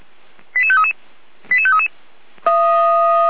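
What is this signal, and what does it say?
Five-tone selective call (ZVEI), heard as quick runs of five tones, tones of about 70 ms. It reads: 88533 88533
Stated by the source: Five-tone ZVEI selective call received by SDR, logged by the fire brigade (Feuerwehr) Orlamünde.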